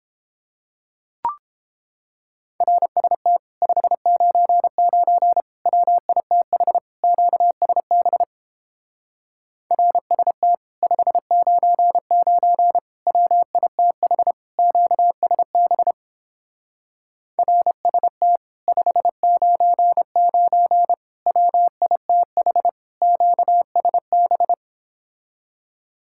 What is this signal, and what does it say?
Morse code, heard as a keyed tone, about 29 wpm, 700 Hz. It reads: RST 599 WITH QSB RST 599 WITH QSB RST 599 WITH QSB